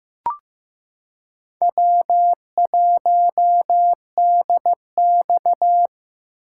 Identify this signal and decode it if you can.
Morse code, heard as a keyed tone, about 15 wpm, 700 Hz. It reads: W1DX
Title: K1USN SST QSO Practice - 15wpm